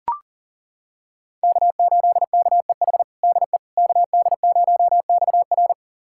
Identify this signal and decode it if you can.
Morse code, heard as a keyed tone, about 40 wpm, 700 Hz. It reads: K8KEH DE KD0XR
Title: USA Callsign DE Callsign - 5 Characters - 40wpm